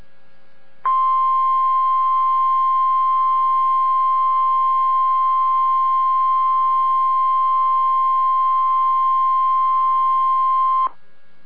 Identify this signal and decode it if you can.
Emergency Alert System (SAME) tones (the 1050 Hz attention tone). It is heard from 0.8 to 10.9 s.